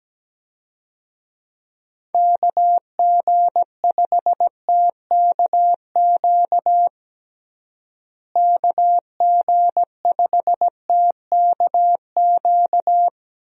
Morse code, heard as a keyed tone, about 17 wpm, 700 Hz. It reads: KG5TKQ KG5TKQ
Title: USA Call Signs - 6 Characters - Repeated 2x - 17wpm